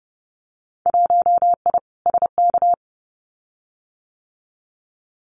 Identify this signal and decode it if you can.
Morse code, heard as a keyed tone, about 30 wpm, 700 Hz. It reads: 1I SK
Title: Winter Field Day Exchanges Encore 30wpm